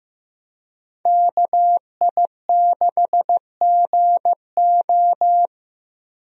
Morse code, heard as a keyed tone, about 15 wpm, 700 Hz. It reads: KI6GO
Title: USA Call Signs - 5 Characters - Repeated 2x - 15wpm